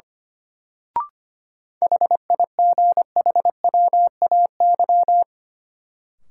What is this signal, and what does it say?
Morse code, heard as a keyed tone, about 25 wpm, 700 Hz. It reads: HIGHWAY
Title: Sentences from Top 1400 Words - New Word Drill - 25wpm